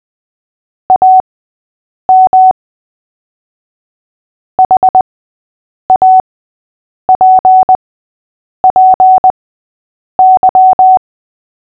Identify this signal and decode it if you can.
Morse code, heard as a keyed tone, about 20 wpm, spaced as for 8 wpm, 750 Hz. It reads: AM HAPPY
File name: cw / fists / Short-Story-8-8wpm